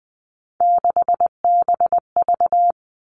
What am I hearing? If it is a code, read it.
Morse code, heard as a keyed tone, about 20 wpm, 700 Hz. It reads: 6BV
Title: Three Challenging Letter-Number Combinations 20wpm